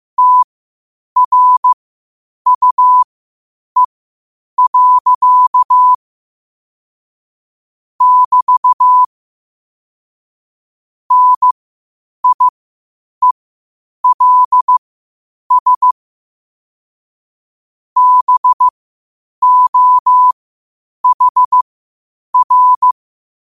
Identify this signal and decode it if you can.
Morse code, heard as a keyed tone, about 15 wpm, spaced as for 8 wpm, 1000 Hz. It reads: TRUE. <BT> NIELS BOHR
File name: qotdfiles / 8wpm